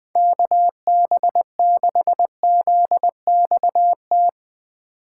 Morse code, heard as a keyed tone, about 20 wpm, 700 Hz. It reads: KB6ZXT